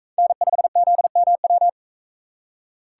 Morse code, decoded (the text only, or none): N5ZMW